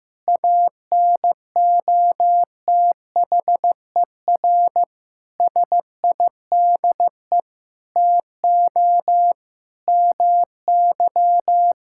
Morse code, heard as a keyed tone, about 16 wpm, 700 Hz. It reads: ANOTHER SIDE TO MY
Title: Sentences from Top 1600 Words 15wpm